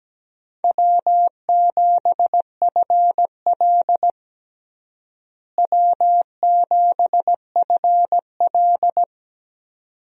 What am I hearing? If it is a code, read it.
Morse code, heard as a keyed tone, about 17 wpm, 700 Hz. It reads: W7FL W7FL
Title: USA Call Signs - 4 Characters - Repeated 2x - 17wpm